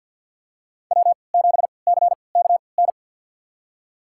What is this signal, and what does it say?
Morse code, heard as a keyed tone, about 53 wpm, spaced as for 29 wpm, 700 Hz. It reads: W7CKN
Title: USA Call Signs - 5 Characters - 3x Character Spacing - 50wpm